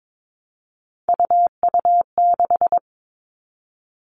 Morse code, heard as a keyed tone, about 22 wpm, 700 Hz. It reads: UU6
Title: All Cut Numbers - 3-Digit 22wpm